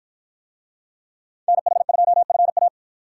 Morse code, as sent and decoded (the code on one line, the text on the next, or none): -. .... ..--- ..-. .-
NH2FA